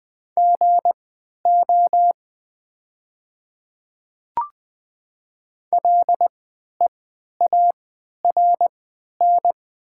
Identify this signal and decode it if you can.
Morse code, heard as a keyed tone, about 20 wpm, spaced as for 11 wpm, 690 Hz. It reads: GO LEARN
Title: 3x Character Spacing - Top 300 Words - 20wpm